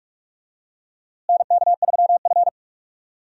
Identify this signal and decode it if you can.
Morse code, heard as a keyed tone, about 46 wpm, 700 Hz. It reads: NK3F
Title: USA Call Signs - 4 Characters - Repeated 3x - 45wpm